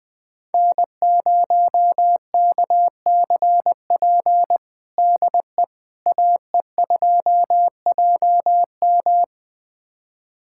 Morse code, heard as a keyed tone, about 21 wpm, 700 Hz. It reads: N0KCP DE AE2JM
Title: USA Callsign DE Callsign 20wpm